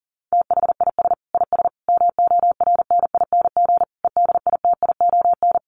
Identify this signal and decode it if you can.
Morse code, heard as a keyed tone, about 40 wpm, 700 Hz. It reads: THIS IS MORNING EDITION
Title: Sets of 5 Words Encore2 40wpm